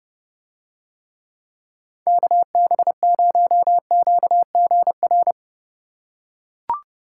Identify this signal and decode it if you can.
Morse code, heard as a keyed tone, about 30 wpm, 700 Hz. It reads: KB0QGR